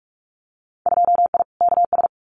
Morse code, heard as a keyed tone, about 46 wpm, 700 Hz. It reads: 2I KS
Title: Winter Field Day Exchanges Encore 45wpm